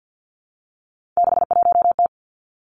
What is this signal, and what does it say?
Morse code, heard as a keyed tone, about 50 wpm, 700 Hz. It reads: 61T
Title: Three Challenging Letter-Number Combinations 50wpm